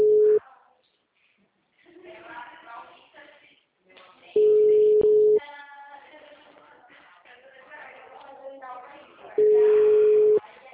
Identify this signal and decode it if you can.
Ringback tone (Europe, 425 Hz), with speech.
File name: out- -2024- - - .23714